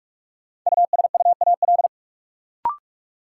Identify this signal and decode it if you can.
Morse code, heard as a keyed tone, about 46 wpm, 700 Hz. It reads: USUAL
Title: Sentences from Top 400 Words - New Word Drill - 45wpm